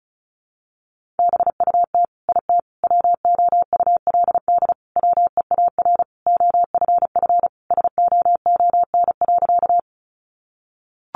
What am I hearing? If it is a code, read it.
Morse code, heard as a keyed tone, about 34 wpm, 700 Hz. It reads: BUT IT WOULD WEAR OFF SOON.
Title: Sentences from Top 1500 Words 35wpm